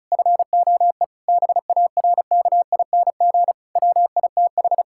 Morse code, heard as a keyed tone, about 34 wpm, 700 Hz. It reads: FOE BARKING WITH
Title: Pangram Challenge v2 - 35wpm